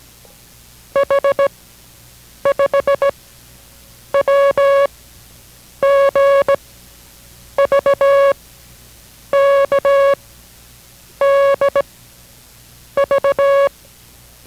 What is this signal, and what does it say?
Morse code, heard as a keyed tone, about 18 wpm, spaced as for 7 wpm, 560 Hz. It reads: H5WGVKDV